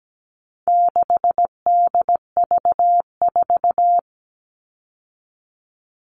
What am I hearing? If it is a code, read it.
Morse code, heard as a keyed tone, about 17 wpm, 700 Hz. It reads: 6DV4
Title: All Cut Numbers - 4-Digit 17wpm